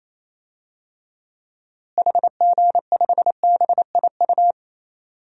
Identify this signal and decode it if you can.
Morse code, heard as a keyed tone, about 28 wpm, 700 Hz. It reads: HG5BIU
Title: DXCC-100 Call Signs - 6 Characters - Repeated 3x - 28wpm